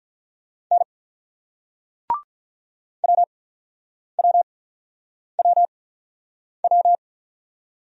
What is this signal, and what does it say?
Morse code, heard as a keyed tone, about 46 wpm, spaced as for 9 wpm, 690 Hz. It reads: N WWWW